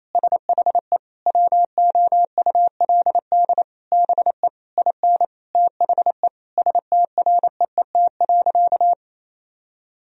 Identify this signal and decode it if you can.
Morse code, heard as a keyed tone, about 27 wpm, 690 Hz. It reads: SHE WOULD BE IN THE STREET.
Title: Sentences from Top 1000 Words - Review Part 3 - 28wpm